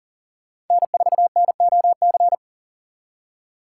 Morse code, heard as a keyed tone, about 40 wpm, 700 Hz. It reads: N4NOC
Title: USA Call Signs - 5 Characters - 2x Character Spacing - 40wpm